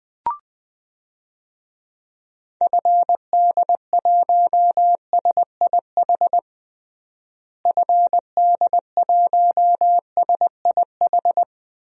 Morse code, heard as a keyed tone, about 21 wpm, 700 Hz. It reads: FD1SIH FD1SIH
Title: DXCC-100 Call Signs - 6 Characters - Repeated 3x - 20wpm